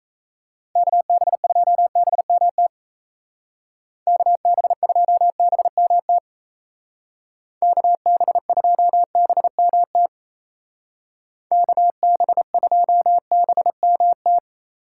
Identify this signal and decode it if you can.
Morse code, heard as a keyed tone, about 36 wpm, 700 Hz. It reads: KB2BMT KB2BMT KB2BMT KB2BMT